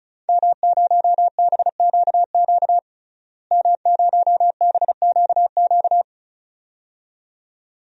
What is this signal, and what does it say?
Morse code, heard as a keyed tone, about 36 wpm, 700 Hz. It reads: M0BQQ M0BQQ
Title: DXCC-100 Call Signs - 5 Characters - Repeated 2x - 35wpm